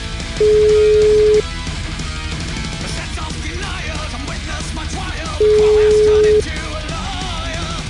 Ringback tone (Europe, 425 Hz).